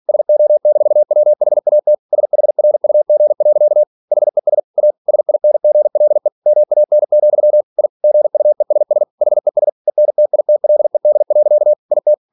Morse code, heard as a keyed tone, about 46 wpm, 600 Hz. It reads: SO-WHAT SHRUG. HES A SINGLE MAN, I GUESS HES ENTITLED. IT
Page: Never 47wpm